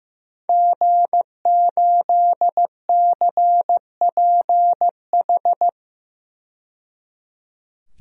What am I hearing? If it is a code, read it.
Morse code, heard as a keyed tone, about 15 wpm, 700 Hz. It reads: G8CPH
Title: DXCC-100 Call Signs - 5 Characters - 15wpm